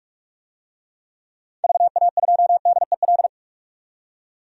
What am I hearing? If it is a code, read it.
Morse code, heard as a keyed tone, about 42 wpm, 700 Hz. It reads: VA2DEL
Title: DXCC-100 Call Signs - 6 Characters - 45wpm